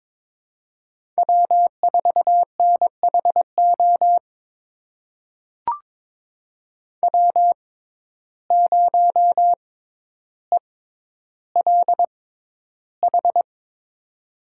Morse code, heard as a keyed tone, about 22 wpm, spaced as for 11 wpm, 700 Hz. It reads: W4NHO W0ELH
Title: USA Call Signs - 5 Characters - 6x Character Spacing - 22wpm